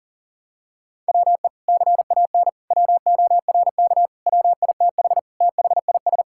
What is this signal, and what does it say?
Morse code, heard as a keyed tone, about 40 wpm, 700 Hz. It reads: WE CAN WORK WITH THIS
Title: Sentences from Top 1000 Words - Review Part 5 - 40wpm